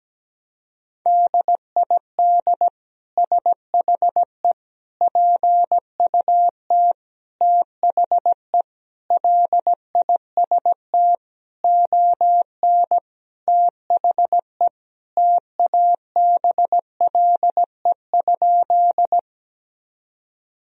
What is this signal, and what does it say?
Morse code, heard as a keyed tone, about 17 wpm, 700 Hz. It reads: DID SHE PUT THE LIST ON THE TABLE?
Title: Sentences from Top 1000 Words - Review Part 2 - 17wpm